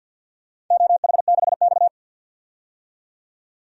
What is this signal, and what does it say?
Morse code, heard as a keyed tone, about 52 wpm, 700 Hz. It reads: OH6X
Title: DXCC-100 Call Signs - 4 Characters - 50wpm